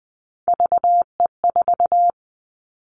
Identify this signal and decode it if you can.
Morse code, heard as a keyed tone, about 20 wpm, 700 Hz. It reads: VE4
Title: Callsign Prefixes for Top 50 DXCC - 20wpm